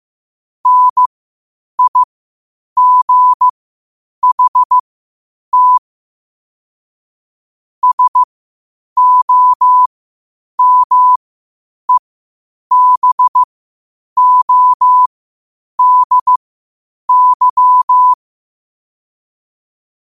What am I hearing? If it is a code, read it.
Morse code, heard as a keyed tone, about 15 wpm, spaced as for 8 wpm, 1000 Hz. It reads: NIGHT SOMEBODY
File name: qotdfiles / 8wpm